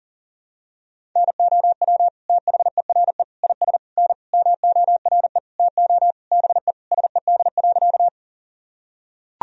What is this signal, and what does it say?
Morse code, heard as a keyed tone, about 40 wpm, 700 Hz. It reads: NOW THERE IS N MORE TO BE SED.
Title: Sentences from Top 100 Words and Abbreviations 40wpm